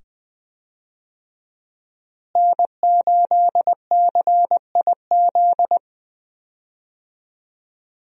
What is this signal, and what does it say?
Morse code, heard as a keyed tone, about 21 wpm, 700 Hz. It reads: N8CIZ
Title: USA Call Signs - 5 Characters - Repeated 2x - 20wpm